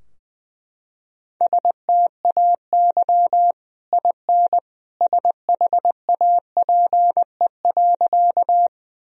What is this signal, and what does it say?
Morse code, heard as a keyed tone, about 21 wpm, 700 Hz. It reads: STAY IN SHAPE.
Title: Sentences from Top 1000 Words - Review Part 3 - 20wpm